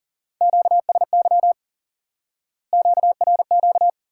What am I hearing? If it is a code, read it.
Morse code, heard as a keyed tone, about 40 wpm, 700 Hz. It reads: QSY QRQ
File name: IQE Course 3 - Lesson 89 - YAGI - Two Elements - 40wpm